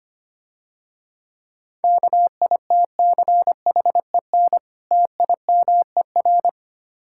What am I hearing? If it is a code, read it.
Morse code, heard as a keyed tone, about 26 wpm, 700 Hz. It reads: KITCHEN TIMER
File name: practice / 25wpm / Difficult Words - Speed-Racing with Hints - 25wpm